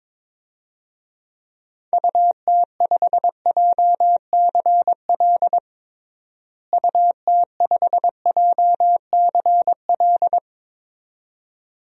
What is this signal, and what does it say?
Morse code, heard as a keyed tone, about 22 wpm, 700 Hz. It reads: UT5JCL UT5JCL
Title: DXCC-100 Call Signs - 6 Characters - Repeated 2x - 22wpm